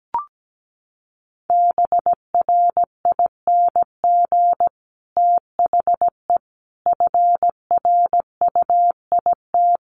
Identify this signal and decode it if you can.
Morse code, heard as a keyed tone, about 17 wpm, 700 Hz. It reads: BRING THE FRUIT
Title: Sentences from Top 1300 Words 17wpm